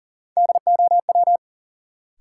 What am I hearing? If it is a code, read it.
Morse code, heard as a keyed tone, about 38 wpm, 700 Hz. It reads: DOW
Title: Sets of 5 Words - New Word Drill - 40wpm